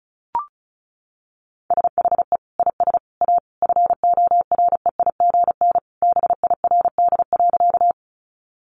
Morse code, heard as a keyed tone, about 34 wpm, 700 Hz. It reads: SHE IS A FOREIGN BIRD.
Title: Sentences from Top 1400 Words 35wpm